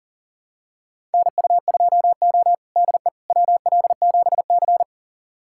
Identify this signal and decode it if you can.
Morse code, heard as a keyed tone, about 40 wpm, 700 Hz. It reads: NU2O DE WL7C
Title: USA Callsign DE Callsign - 4 Characters - 40wpm